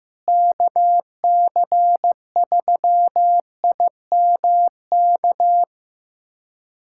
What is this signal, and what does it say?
Morse code, heard as a keyed tone, about 15 wpm, 700 Hz. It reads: KC3IMK
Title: USA Callsign DE Callsign - 6 Characters - 15wpm